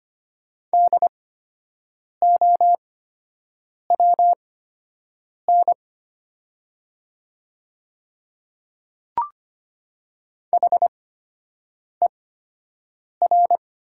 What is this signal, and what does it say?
Morse code, heard as a keyed tone, about 25 wpm, spaced as for 7 wpm, 700 Hz. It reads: DOWN HER
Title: 8x Character Spacing - Top 300 Words - 25wpm